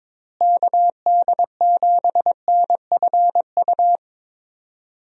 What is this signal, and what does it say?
Morse code, heard as a keyed tone, about 22 wpm, 700 Hz. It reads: KD7NFU
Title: USA Call Signs - 6 Characters - Repeated 2x - 22wpm